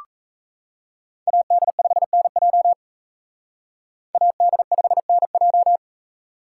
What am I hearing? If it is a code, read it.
Morse code, heard as a keyed tone, about 37 wpm, 700 Hz. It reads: AD5NJ AD5NJ